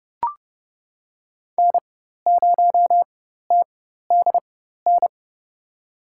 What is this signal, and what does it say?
Morse code, heard as a keyed tone, about 31 wpm, spaced as for 14 wpm, 700 Hz. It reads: N0TDN
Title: USA Call Signs - 5 Characters - 4x Character Spacing - 30wpm